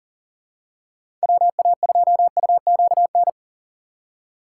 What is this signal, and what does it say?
Morse code, heard as a keyed tone, about 40 wpm, 700 Hz. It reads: WA2UQN
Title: USA Call Signs - 6 Characters - 3x Character Spacing - 40wpm